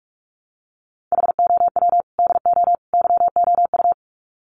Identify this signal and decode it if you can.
Morse code, heard as a keyed tone, about 44 wpm, 700 Hz. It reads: HOW DO YOU
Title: Sets of 3 Words 45wpm